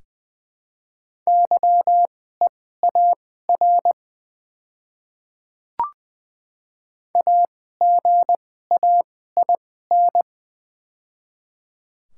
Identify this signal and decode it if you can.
Morse code, heard as a keyed tone, about 20 wpm, spaced as for 15 wpm, 700 Hz. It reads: YEAR AGAIN
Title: 2x Character Spacing - Top 200 Words - 20wpm